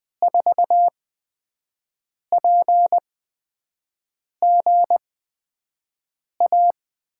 Morse code, heard as a keyed tone, about 20 wpm, spaced as for 5 wpm, 700 Hz. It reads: 4PGA
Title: USA Call Signs - 6 Characters - 8x Character Spacing - 20wpm